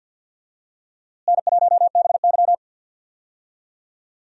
Morse code, heard as a keyed tone, about 48 wpm, 700 Hz. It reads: N1BY
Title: USA Call Signs - 4 Characters - Repeated 2x - 50wpm